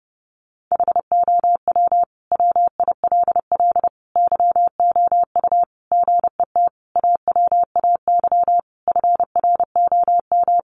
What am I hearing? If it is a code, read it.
Morse code, heard as a keyed tone, about 30 wpm, 700 Hz. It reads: HOW WILL YOU GET AWAY FROM
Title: Sentences from Top 1500 Words 30wpm